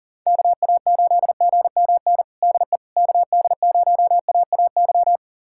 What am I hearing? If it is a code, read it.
Morse code, heard as a keyed tone, about 40 wpm, 700 Hz. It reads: KA8GMN DE KD0AAY
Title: USA Callsign DE Callsign 40wpm